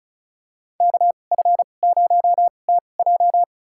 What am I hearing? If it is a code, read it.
Morse code, heard as a keyed tone, about 34 wpm, spaced as for 25 wpm, 700 Hz. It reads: KF0TJ
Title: USA Call Signs - 5 Characters - 2x Character Spacing - 35wpm